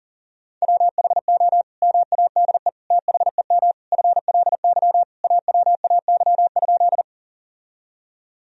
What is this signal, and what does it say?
Morse code, heard as a keyed tone, about 40 wpm, 700 Hz. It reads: WHO MADE THEM FLY AWAY?
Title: Sentences from Top 1000 Words - Review Part 2 - 40wpm